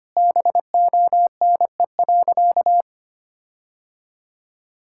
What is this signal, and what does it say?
Morse code, heard as a keyed tone, about 25 wpm, 700 Hz. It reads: BONE.
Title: Sentences from Top 1600 Words 25wpm